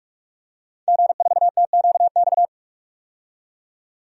Morse code, heard as a keyed tone, about 44 wpm, 690 Hz. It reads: G4TQX
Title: DXCC-100 Call Signs - 5 Characters - 45wpm